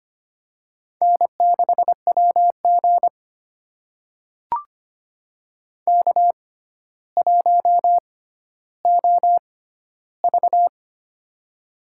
Morse code, heard as a keyed tone, about 25 wpm, spaced as for 13 wpm, 700 Hz. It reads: N6WG K1OV
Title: USA Call Signs - 4 Characters - 6x Character Spacing - 25wpm